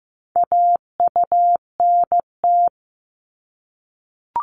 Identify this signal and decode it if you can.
Morse code, heard as a keyed tone, about 15 wpm, 700 Hz. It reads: AUNT
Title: Multi-letter Phonemes Repeated 3x 15wpm